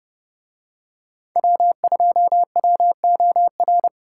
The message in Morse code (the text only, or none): W2WOR